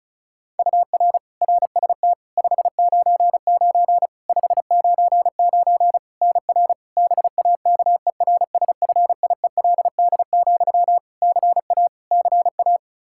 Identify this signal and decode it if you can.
Morse code, heard as a keyed tone, about 36 wpm, 700 Hz. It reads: UR RST 599 599 NR BAKERSFIELD, CA CA
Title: Practice QSO - Exchange 1 - Beginning - 35wpm